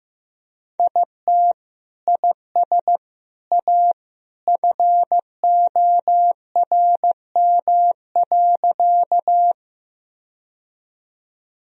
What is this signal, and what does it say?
Morse code, heard as a keyed tone, about 16 wpm, 700 Hz. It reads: IT IS A FORM.
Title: Sentences from Top 1000 Words - Review Part 1 - 15wpm